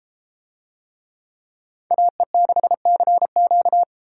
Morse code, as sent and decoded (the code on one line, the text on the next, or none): .- . -.... -.-. --.-
AE6CQ